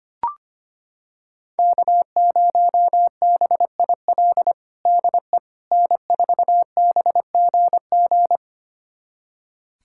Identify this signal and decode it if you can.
Morse code, heard as a keyed tone, about 25 wpm, 700 Hz. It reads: K0BIL DE N4BGG
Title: USA Callsign DE Callsign - 5 Characters - 25wpm